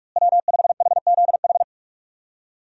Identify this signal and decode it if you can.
Morse code, heard as a keyed tone, about 46 wpm, 700 Hz. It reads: W5HZH
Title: USA Call Signs - 5 Characters - Repeated 3x - 45wpm